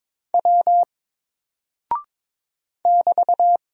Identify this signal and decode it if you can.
Morse code, heard as a keyed tone, about 22 wpm, 700 Hz. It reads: W <BT>
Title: Single Character (Letter:Number:Punctuation:Prosign) 22wpm